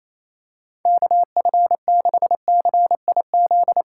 Morse code, heard as a keyed tone, about 28 wpm, 700 Hz. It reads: KF6CIZ
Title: USA Call Signs - 6 Characters - 3x Character Spacing - 28wpm